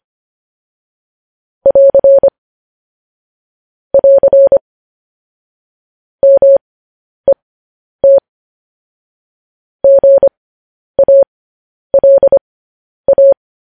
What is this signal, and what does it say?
Morse code, heard as a keyed tone, about 25 wpm, spaced as for 10 wpm, 550 Hz. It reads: <AR> <AR> MET GALA
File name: podcast / News headlines-1x-25x10